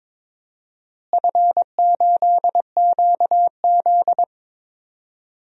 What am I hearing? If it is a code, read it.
Morse code, heard as a keyed tone, about 22 wpm, 700 Hz. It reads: F8QZ
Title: DXCC-100 Call Signs - 4 Characters - Repeated 2x - 22wpm